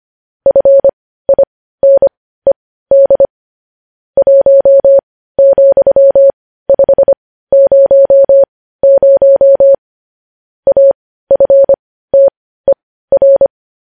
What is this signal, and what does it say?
Morse code, heard as a keyed tone, about 25 wpm, spaced as for 15 wpm, 550 Hz. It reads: FINED 1,500 AFTER
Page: News headlines in Morse code at 15 wpm